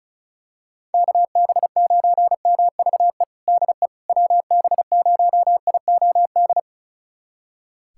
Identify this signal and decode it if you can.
Morse code, heard as a keyed tone, about 35 wpm, 700 Hz. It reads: KB9MVE DE WB0IOD
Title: USA Callsign DE Callsign 35wpm